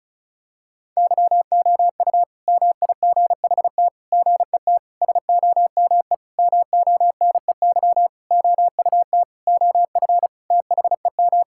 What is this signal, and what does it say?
Morse code, heard as a keyed tone, about 34 wpm, 700 Hz. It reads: YOU MIGHT GET SOME MONEY OUT OF THEM
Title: Sentences from Top 1000 Words - Review Part 5 - 35wpm